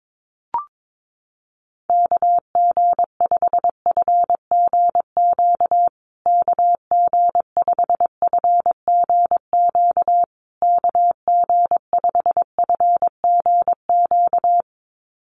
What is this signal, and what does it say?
Morse code, heard as a keyed tone, about 22 wpm, 700 Hz. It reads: KG5FGQ KG5FGQ KG5FGQ
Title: USA Call Signs Repeated 3x 22wpm